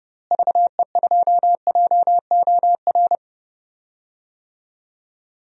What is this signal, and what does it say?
Morse code, heard as a keyed tone, about 31 wpm, 700 Hz. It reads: VE2JOR
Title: DXCC-100 Call Signs - 6 Characters - Repeated 2x - 30wpm